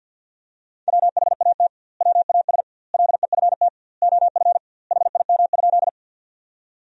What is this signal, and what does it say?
Morse code, heard as a keyed tone, about 50 wpm, 700 Hz. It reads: WHAT WAS LEFT OF HIM?
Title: 2x Word-Spacing - Sentences from Top 300 Words - 50wpm